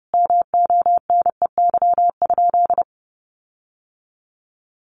Morse code, heard as a keyed tone, about 30 wpm, 700 Hz. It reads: MONEY?
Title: Sentences from Top 1400 Words 30wpm